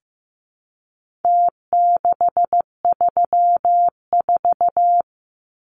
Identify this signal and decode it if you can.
Morse code, heard as a keyed tone, about 15 wpm, 700 Hz. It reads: T634
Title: Numbers - Contest Style - 4-Digit 15wpm